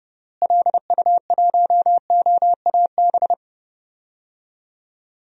Morse code, heard as a keyed tone, about 30 wpm, 700 Hz. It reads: LU1OAB
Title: DXCC-100 Call Signs - 6 Characters - Repeated 3x - 30wpm